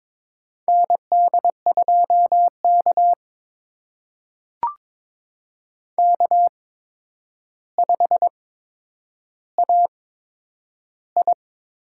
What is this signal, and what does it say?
Morse code, heard as a keyed tone, about 21 wpm, spaced as for 9 wpm, 700 Hz. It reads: ND2K K5AI
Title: USA Call Signs - 4 Characters - 8x Character Spacing - 22wpm